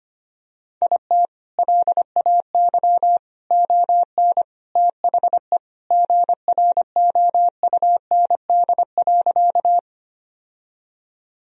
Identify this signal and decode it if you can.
Morse code, heard as a keyed tone, about 25 wpm, 700 Hz. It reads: IT LAY ON THE GROUND.
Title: Sentences from Top 1000 Words - Review Part 2 - 25wpm